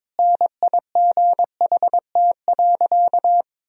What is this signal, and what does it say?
Morse code, heard as a keyed tone, about 22 wpm, 700 Hz. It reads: NIGHT.
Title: Sentences from Top 500 Words and Abbreviations 22wpm